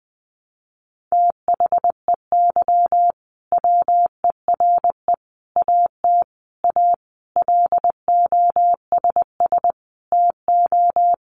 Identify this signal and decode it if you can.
Morse code, heard as a keyed tone, about 20 wpm, 700 Hz. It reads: THEY WERE AT A LOSS TO
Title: Sentences from Top 1400 Words 20wpm